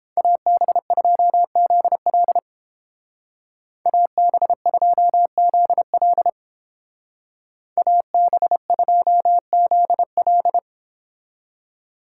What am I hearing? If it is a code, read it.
Morse code, heard as a keyed tone, about 30 wpm, 700 Hz. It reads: AB2ZL AB2ZL AB2ZL